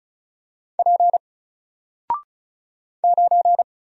Morse code, heard as a keyed tone, about 34 wpm, 700 Hz. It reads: P 9
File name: Single-Letter-Number-35wpm